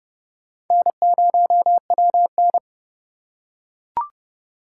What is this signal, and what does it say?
Morse code, heard as a keyed tone, about 30 wpm, 700 Hz. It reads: N0WN
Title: USA Call Signs - 4 Characters - 2x Character Spacing - 30wpm